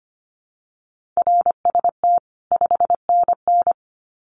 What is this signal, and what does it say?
Morse code, heard as a keyed tone, about 25 wpm, 700 Hz. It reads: RST 5NN
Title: Common QSO Phrases 25wpm